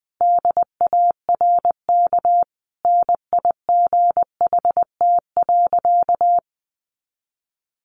Morse code, heard as a keyed tone, about 20 wpm, 700 Hz. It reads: DARK NIGHT.
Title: Sentences from Top 500 Words 20wpm